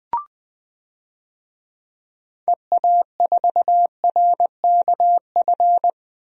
Morse code, heard as a keyed tone, about 20 wpm, 700 Hz. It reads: EA4RKF
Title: DXCC-100 Call Signs - 6 Characters - Repeated 3x - 20wpm